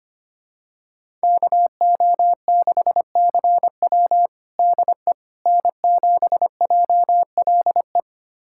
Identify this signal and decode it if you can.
Morse code, heard as a keyed tone, about 25 wpm, 700 Hz. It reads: KO6CW DE N7JLE
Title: USA Callsign DE Callsign - 5 Characters - 25wpm